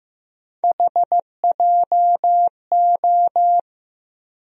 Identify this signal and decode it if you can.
Morse code, heard as a keyed tone, about 15 wpm, 700 Hz. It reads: HJO
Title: USA Call Sign Postfixes - 3 Letters 15wpm